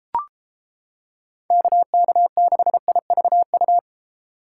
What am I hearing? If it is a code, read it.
Morse code, heard as a keyed tone, about 31 wpm, 700 Hz. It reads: KK6IVU